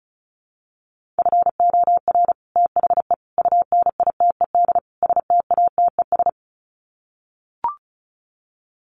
Morse code, heard as a keyed tone, about 34 wpm, 700 Hz. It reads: FOR THE UNITED STATES
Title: Sets of 4 Words 35wpm